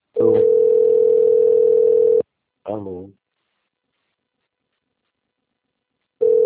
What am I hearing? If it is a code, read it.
Ringback tone (North America), 440 + 480 Hz, with speech.
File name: rg-1600-unknown- - - .24110